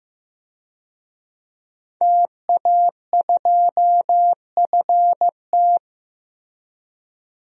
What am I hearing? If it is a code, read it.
Morse code, heard as a keyed tone, about 16 wpm, 700 Hz. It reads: TA2FT
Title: DXCC-100 Call Signs - 5 Characters - Repeated 2x - 15wpm